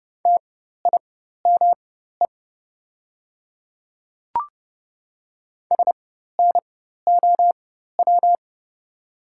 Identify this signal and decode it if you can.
Morse code, heard as a keyed tone, about 29 wpm, spaced as for 14 wpm, 700 Hz. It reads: TIME SNOW
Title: 4x Character Spacing - Top 500 Words - 30wpm